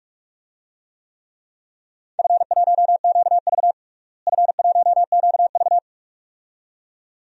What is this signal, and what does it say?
Morse code, heard as a keyed tone, about 46 wpm, 700 Hz. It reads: F1QV F1QV
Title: DXCC-100 Call Signs - 4 Characters - Repeated 2x - 45wpm